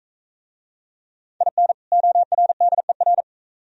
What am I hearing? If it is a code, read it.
Morse code, heard as a keyed tone, about 41 wpm, 700 Hz. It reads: IN ORDER